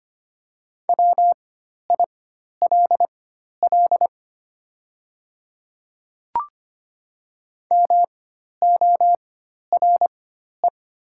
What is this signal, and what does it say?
Morse code, heard as a keyed tone, about 25 wpm, spaced as for 12 wpm, 700 Hz. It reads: WILL MORE
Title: 4x Character Spacing - Top 100 Words - 25wpm